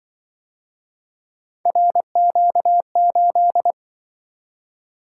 Morse code, heard as a keyed tone, about 25 wpm, 700 Hz. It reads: RQ8